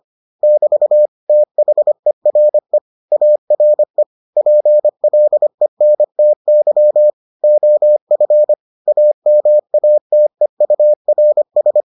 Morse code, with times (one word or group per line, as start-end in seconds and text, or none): 0.4-1.1 <BT>
1.3-2.8 THERE
3.1-4.0 ARE
4.4-7.1 PLENTY
7.4-8.5 OF
8.9-11.8 AMATEURS